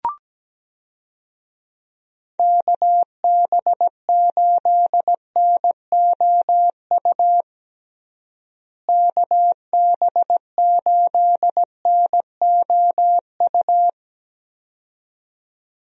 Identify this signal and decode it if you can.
Morse code, heard as a keyed tone, about 17 wpm, 700 Hz. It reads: KB8NOU KB8NOU
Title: USA Call Signs - 6 Characters - Repeated 2x - 17wpm